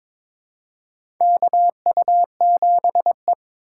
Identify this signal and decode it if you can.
Morse code, heard as a keyed tone, about 21 wpm, 700 Hz. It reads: KU7E
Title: USA Call Signs - 4 Characters - 3x Character Spacing - 22wpm